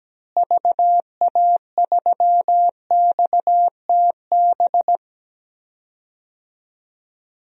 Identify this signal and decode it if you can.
Morse code, heard as a keyed tone, about 17 wpm, 700 Hz. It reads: VA3XTB
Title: DXCC-100 Call Signs - 6 Characters - Repeated 3x - 17wpm